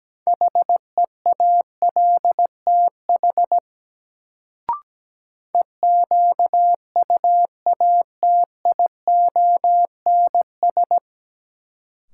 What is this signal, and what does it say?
Morse code, heard as a keyed tone, about 17 wpm, 700 Hz. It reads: HEALTH EQUATIONS
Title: Sets of 5 Words - New Word Drill - 17wpm